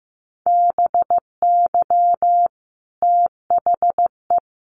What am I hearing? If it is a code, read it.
Morse code, heard as a keyed tone, about 15 wpm, 700 Hz. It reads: BY THE
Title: Sentences from Top 1200 Words 15wpm